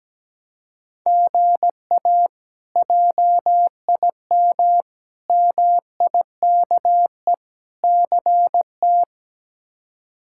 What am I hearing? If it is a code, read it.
Morse code, heard as a keyed tone, about 17 wpm, 700 Hz. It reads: GA JIM MIKE CT
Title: K1USN SST QSO Practice - 17wpm